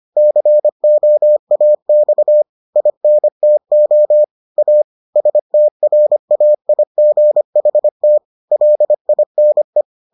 Morse code, heard as a keyed tone, about 25 wpm, 600 Hz. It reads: COAX INTO A STRAIGHT LINE